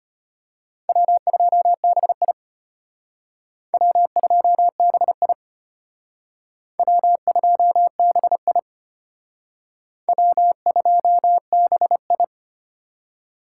Morse code, heard as a keyed tone, about 32 wpm, 700 Hz. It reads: W2BI W2BI W2BI W2BI